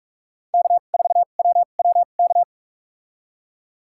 Morse code, heard as a keyed tone, about 41 wpm, spaced as for 31 wpm, 700 Hz. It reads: K4WWK
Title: USA Call Signs - 5 Characters - 2x Character Spacing - 45wpm